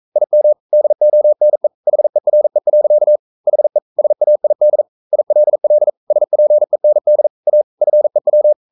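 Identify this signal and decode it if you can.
Morse code, heard as a keyed tone, about 42 wpm, 600 Hz. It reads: IM DONE HERE. HE SAID ILL SPEND A FEW
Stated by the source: Morse code sending of an English novel at 42 wpm.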